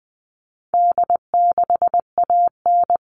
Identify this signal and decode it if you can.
Morse code, heard as a keyed tone, about 20 wpm, 700 Hz. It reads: D6AN
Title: All Cut Numbers - 4-Digit 20wpm